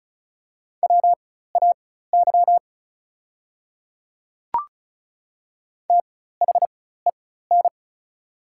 Morse code, heard as a keyed tone, about 33 wpm, spaced as for 16 wpm, 690 Hz. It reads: WAY THEN
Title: 4x Character Spacing - Top 100 Words - 35wpm